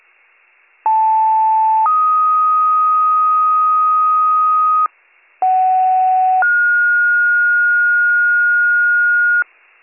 Two-tone page, heard as a one-second tone, then a three-second tone, about 880 then 1250 Hz, about 750 then 1430 Hz, with noise. It is heard from 0.9 to 9.4 s.